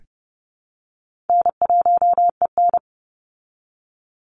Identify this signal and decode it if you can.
Morse code, heard as a keyed tone, about 30 wpm, 700 Hz. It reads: N1EN
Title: Call Signs Repeated 2x - 30wpm